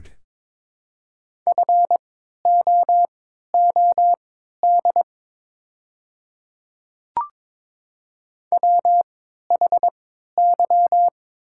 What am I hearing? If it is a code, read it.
Morse code, heard as a keyed tone, about 22 wpm, spaced as for 12 wpm, 700 Hz. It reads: FOOD WHY
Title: 3x Character Spacing - Top 500 Words - 22wpm